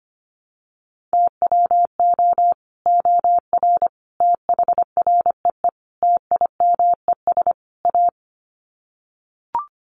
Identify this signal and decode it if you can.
Morse code, heard as a keyed tone, about 25 wpm, 700 Hz. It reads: TWO OR THREE TIMES A
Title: Sets of 5 Words Encore2 25wpm